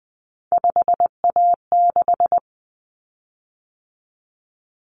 Morse code, heard as a keyed tone, about 20 wpm, 700 Hz. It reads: 5A6
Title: Common Cut Numbers - 3-Digit 20wpm